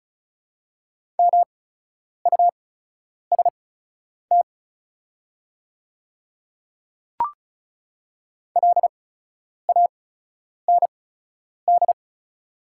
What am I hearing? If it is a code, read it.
Morse code, heard as a keyed tone, about 36 wpm, spaced as for 10 wpm, 700 Hz. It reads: MUST LAND
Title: 8x Character Spacing - Top 200 Words - 35wpm